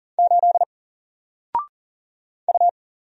Morse code, heard as a keyed tone, about 39 wpm, 700 Hz. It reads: 8 U
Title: Single Letter-Number Alt 40wpm